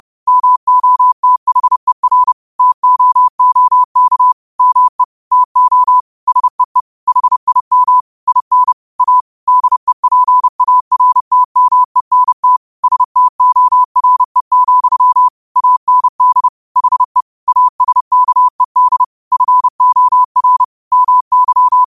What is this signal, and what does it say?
Morse code, heard as a keyed tone, about 31 wpm, 1000 Hz. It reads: MOTHER TOOK ME TO SEE HIM IN A DEPARTMENT STORE, AND HE ASKED FOR MY
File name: qotdfiles / 30wpm